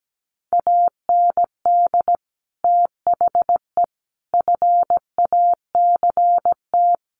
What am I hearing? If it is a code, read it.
Morse code, heard as a keyed tone, about 17 wpm, 700 Hz. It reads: AND THE FACT
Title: Sets of 4 Words 17wpm